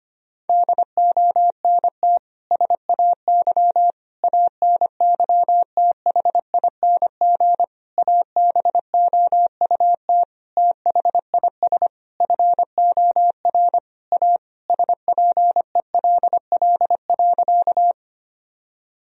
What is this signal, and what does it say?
Morse code, heard as a keyed tone, about 25 wpm, 700 Hz. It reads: DONT SAY ANYTHING ABOUT THIS FOR A SPELL.
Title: Sentences from Top 1000 Words - Review Part 1 - 25wpm